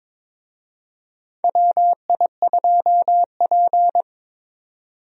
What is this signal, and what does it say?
Morse code, heard as a keyed tone, about 22 wpm, 700 Hz. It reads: WI2P
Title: USA Call Signs - 4 Characters - 22wpm